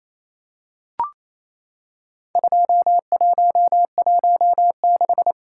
Morse code, heard as a keyed tone, about 28 wpm, 700 Hz. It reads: 2116
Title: Worked All Europe DX Contest - QTC Exchange - 28wpm